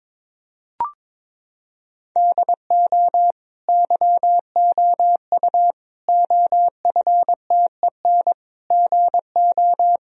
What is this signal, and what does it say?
Morse code, heard as a keyed tone, about 22 wpm, 700 Hz. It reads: DO YOU OFTEN GO
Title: Sentences from Top 1000 Words - Review Part 1 - 22wpm